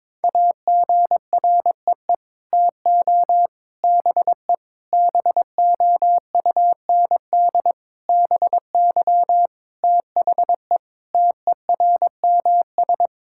Morse code, heard as a keyed tone, about 22 wpm, 700 Hz. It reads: AGREE TO BE BOUND BY THE TERMS
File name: Sets of 9 Words - 22wpm